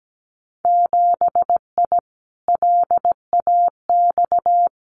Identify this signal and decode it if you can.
Morse code, heard as a keyed tone, about 17 wpm, 700 Hz. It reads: 7I LAX
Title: Winter Field Day Exchanges Encore 17wpm